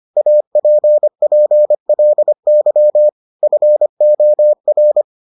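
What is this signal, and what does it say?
Morse code, heard as a keyed tone, about 25 wpm, 600 Hz. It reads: APPLY FOR
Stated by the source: Morse edition of the Foundations of Amateur Radio episodes.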